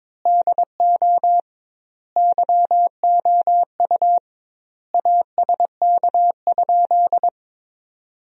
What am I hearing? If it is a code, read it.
Morse code, heard as a keyed tone, about 22 wpm, 700 Hz. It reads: DO YOU ASK?
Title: 2x Word-Spacing - Sentences from Top 200 Words - 22wpm